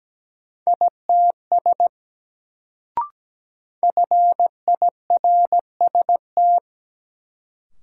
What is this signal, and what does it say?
Morse code, heard as a keyed tone, about 17 wpm, 700 Hz. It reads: ITS FIRST
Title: Sets of 2 Words - New Word Drill - 17wpm